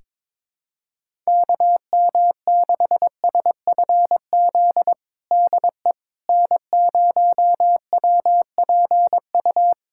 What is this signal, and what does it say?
Morse code, heard as a keyed tone, about 22 wpm, 700 Hz. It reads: KM6SFZ DE N0WPU
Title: USA Callsign DE Callsign 22wpm